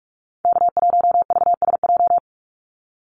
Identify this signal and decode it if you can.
Morse code, heard as a keyed tone, about 44 wpm, 700 Hz. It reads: K1VSJ
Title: Call Signs - 45wpm